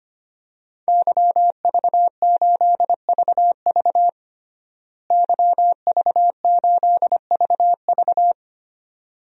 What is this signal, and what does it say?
Morse code, heard as a keyed tone, about 25 wpm, 700 Hz. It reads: YV8VV YV8VV
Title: DXCC-100 Call Signs - 5 Characters - Repeated 2x - 25wpm